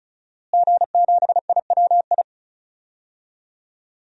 Morse code, heard as a keyed tone, about 34 wpm, 700 Hz. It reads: G7IWI
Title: DXCC-100 Call Signs - 5 Characters - Repeated 3x - 35wpm